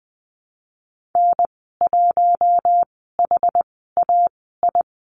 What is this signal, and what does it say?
Morse code, heard as a keyed tone, about 19 wpm, spaced as for 14 wpm, 700 Hz. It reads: N1HAI
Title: 238 Lesson 38g - Letter q - Call Signs 20wpm